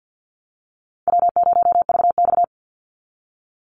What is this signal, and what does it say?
Morse code, heard as a keyed tone, about 50 wpm, 700 Hz. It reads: W0VX
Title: Easy Call Signs Repeated 2x - 50wpm